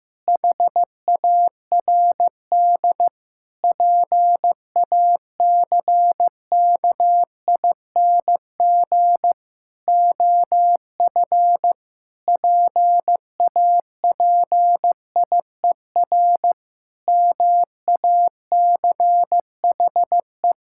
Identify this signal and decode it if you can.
Morse code, heard as a keyed tone, about 15 wpm, 700 Hz. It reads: HARD PACKING OF PAPIER MACHE
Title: Pangram Challenge v2 - 15wpm